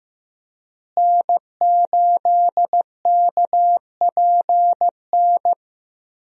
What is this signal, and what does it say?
Morse code, heard as a keyed tone, about 16 wpm, 700 Hz. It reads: N8KPN
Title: USA Call Signs - 5 Characters - Repeated 3x - 15wpm